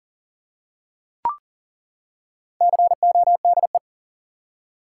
Morse code, heard as a keyed tone, about 40 wpm, 700 Hz. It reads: CODE
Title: Four Letter Words - 40wpm